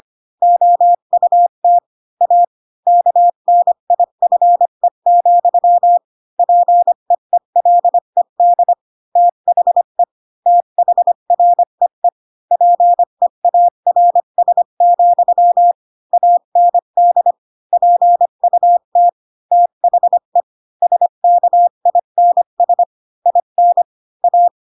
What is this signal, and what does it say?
Morse code, heard as a keyed tone, about 25 wpm, 700 Hz. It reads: OUT A KNIFE, PEELED THE THREE PEARS, AND PUT THE SKINS IN A
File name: Pinocchio / Pinocchio ch7 20 25 ik1hge